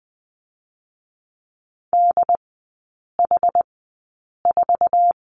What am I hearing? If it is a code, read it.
Morse code, heard as a keyed tone, about 20 wpm, spaced as for 8 wpm, 700 Hz. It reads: DH4
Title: 067 Lesson 12e - Letter d - Three Characters 20wpm